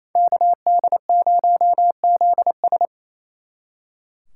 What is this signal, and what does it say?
Morse code, heard as a keyed tone, about 36 wpm, 700 Hz. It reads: KD0ZS